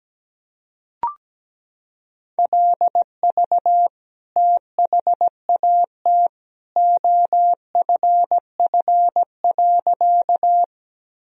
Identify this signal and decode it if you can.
Morse code, heard as a keyed tone, about 17 wpm, 700 Hz. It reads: LV THAT OFF.
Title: Sentences from Top 400 Words and Abbreviations 17wpm